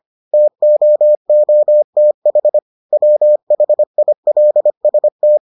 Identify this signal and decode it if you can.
Morse code, heard as a keyed tone, about 25 wpm, 600 Hz. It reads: TOOTH WHILST